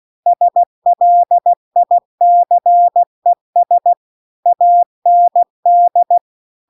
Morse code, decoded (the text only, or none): SLICES AND